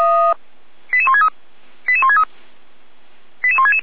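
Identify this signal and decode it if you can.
Five-tone selective call (ZVEI), heard as quick runs of five tones, tones of about 70 ms. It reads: 88163 88163 88166
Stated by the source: ZVEI five-tone selective call received on an SDR.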